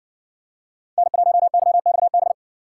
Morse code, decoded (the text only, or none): N1YXB